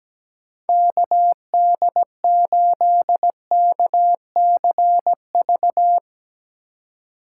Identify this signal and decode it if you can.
Morse code, heard as a keyed tone, about 17 wpm, 700 Hz. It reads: KD8KCV
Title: USA Call Signs - 2x Character Spacing - 17wpm